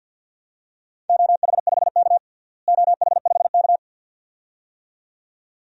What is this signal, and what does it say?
Morse code, heard as a keyed tone, about 48 wpm, 700 Hz. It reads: OH5K OH5K
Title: DXCC-100 Call Signs - 4 Characters - Repeated 2x - 50wpm